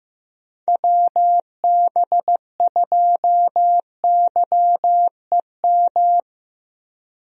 Morse code, heard as a keyed tone, about 15 wpm, 700 Hz. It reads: WB2YEM
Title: USA Call Signs - 6 Characters - Repeated 3x - 15wpm